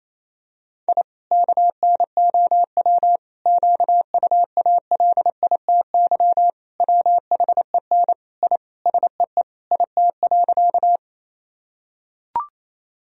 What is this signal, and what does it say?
Morse code, heard as a keyed tone, about 28 wpm, 700 Hz. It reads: I KNOW QUALITY WHEN I SEE IT.
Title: Sentences from Top 1000 Words - Review Part 4 - 28wpm